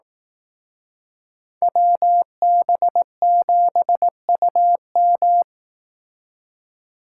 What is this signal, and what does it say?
Morse code, heard as a keyed tone, about 18 wpm, 700 Hz. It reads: WB7UM
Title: Speed Racing - USA 2x2 Call Signs 15wpm